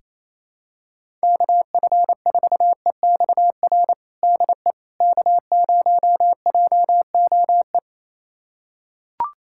Morse code, heard as a keyed tone, about 28 wpm, 700 Hz. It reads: KF4EXR DE K0JOE
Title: USA Callsign DE Callsign 28wpm